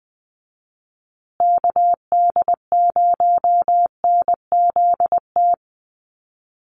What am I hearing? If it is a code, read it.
Morse code, heard as a keyed tone, about 20 wpm, 700 Hz. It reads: KD0NZT
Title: USA Call Signs Repeated 3x 20wpm